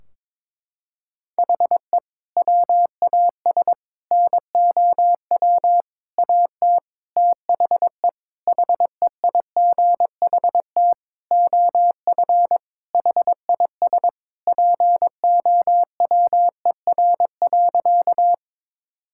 Morse code, heard as a keyed tone, about 21 wpm, 700 Hz. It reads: HE WAS NOW AT THE HEIGHT OF HIS POWER.
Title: Sentences from Top 1600 Words 22wpm